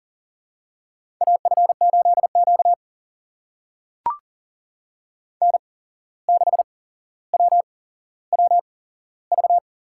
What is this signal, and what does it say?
Morse code, heard as a keyed tone, about 40 wpm, spaced as for 16 wpm, 700 Hz. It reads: AF8Q N6WWV
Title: USA Call Signs - 8x Character Spacing - 40wpm